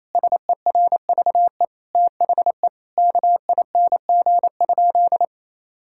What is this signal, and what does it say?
Morse code, heard as a keyed tone, about 28 wpm, 700 Hz. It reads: SERVE THE KING?
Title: Sentences from Top 1600 Words 28wpm